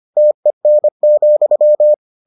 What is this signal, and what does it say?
Morse code, heard as a keyed tone, about 25 wpm, 600 Hz. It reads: TEN,